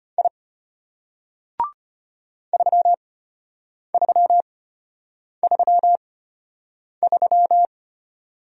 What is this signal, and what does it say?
Morse code, heard as a keyed tone, about 33 wpm, spaced as for 8 wpm, 700 Hz. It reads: I 3333